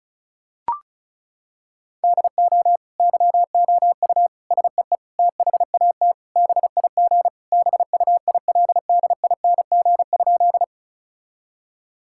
Morse code, heard as a keyed tone, about 34 wpm, 700 Hz. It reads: DO YOU SEE THAT BIG BUILDING?
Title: Sentences from Top 1000 Words - Review Part 5 - 35wpm